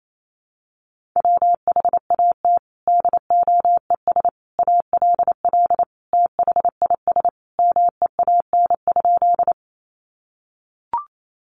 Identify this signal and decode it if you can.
Morse code, heard as a keyed tone, about 28 wpm, 700 Hz. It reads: WHAT DOES ALL THIS MEAN?
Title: Sentences from Top 200 Words 28wpm